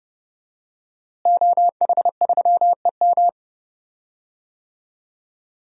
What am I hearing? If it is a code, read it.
Morse code, heard as a keyed tone, about 30 wpm, 700 Hz. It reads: OH3EM
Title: DXCC-100 Call Signs - 5 Characters - Repeated 3x - 30wpm